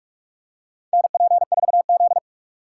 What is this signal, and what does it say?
Morse code, heard as a keyed tone, about 44 wpm, 700 Hz. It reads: NP4Z
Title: USA Call Signs - 4 Characters - 3x Character Spacing - 45wpm